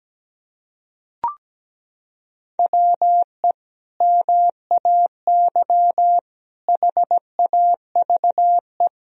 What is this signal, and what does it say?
Morse code, heard as a keyed tone, about 17 wpm, 700 Hz. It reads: WE MAY HAVE
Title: Sentences from Top 1000 Words - Review Part 1 - 17wpm